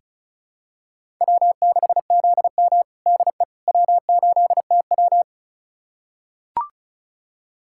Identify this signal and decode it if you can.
Morse code, heard as a keyed tone, about 36 wpm, 700 Hz. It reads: W6ZM DE W8TW
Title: USA Callsign DE Callsign - 4 Characters - 35wpm